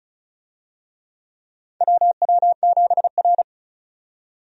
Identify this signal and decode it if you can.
Morse code, heard as a keyed tone, about 34 wpm, 700 Hz. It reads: WW7R
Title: USA Call Signs - 4 Characters - 35wpm